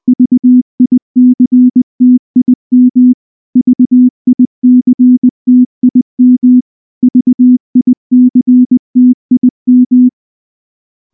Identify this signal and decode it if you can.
Morse code, heard as a keyed tone, about 20 wpm, 250 Hz. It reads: VICTIM VICTIM VICTIM
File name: morse-training-20WPM